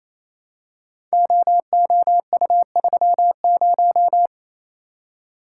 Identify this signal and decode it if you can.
Morse code, heard as a keyed tone, about 29 wpm, 700 Hz. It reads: OOU30
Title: Code Groups - 5 Letter:Numbers - 28wpm